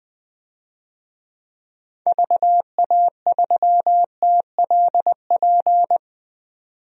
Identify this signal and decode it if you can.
Morse code, heard as a keyed tone, about 20 wpm, 700 Hz. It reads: VA3TLP
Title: DXCC-100 Call Signs - 6 Characters - 20wpm